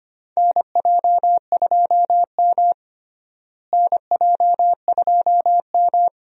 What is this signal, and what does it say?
Morse code, heard as a keyed tone, about 25 wpm, 700 Hz. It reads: NJ2M NJ2M
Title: USA Call Signs - 4 Characters - Repeated 2x - 25wpm